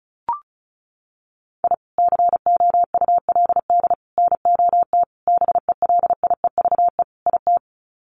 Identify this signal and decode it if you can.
Morse code, heard as a keyed tone, about 34 wpm, 700 Hz. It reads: I COULD NOT BELIEVE IT
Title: Sets of 5 Words Encore 35wpm